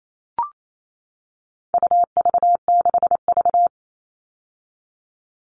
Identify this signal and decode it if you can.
Morse code, heard as a keyed tone, about 28 wpm, 700 Hz. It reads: UV6V
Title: All Cut Numbers - 4-Digit 28wpm